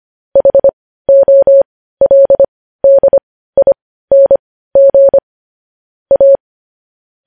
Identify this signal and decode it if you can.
Morse code, heard as a keyed tone, about 25 wpm, spaced as for 15 wpm, 550 Hz. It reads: HOLDING A
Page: news headlines in morse code at 15 wpm